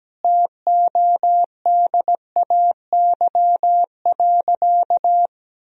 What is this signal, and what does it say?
Morse code, heard as a keyed tone, about 17 wpm, 700 Hz. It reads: TODAY.
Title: Sentences from Top 1000 Words - Review Part 3 - 17wpm